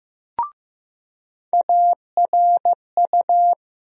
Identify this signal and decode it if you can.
Morse code, heard as a keyed tone, about 15 wpm, 700 Hz. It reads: ARU
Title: USA Call Sign Postfixes - 3 Letters 15wpm